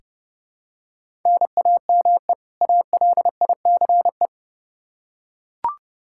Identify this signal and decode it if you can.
Morse code, heard as a keyed tone, about 30 wpm, 700 Hz. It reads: NAME ALICE
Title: Practice QSO - Exchange 1 - Beginning - 30wpm